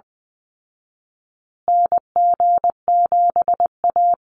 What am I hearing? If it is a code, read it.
Morse code, heard as a keyed tone, about 20 wpm, 700 Hz. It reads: NG7A
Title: Call Signs Repeated 3x - 20wpm